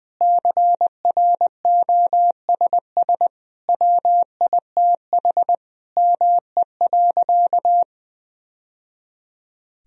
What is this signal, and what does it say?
Morse code, heard as a keyed tone, about 20 wpm, 700 Hz. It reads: CROSS WITH ME.
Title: Sentences from Top 1000 Words - Review Part 1 - 20wpm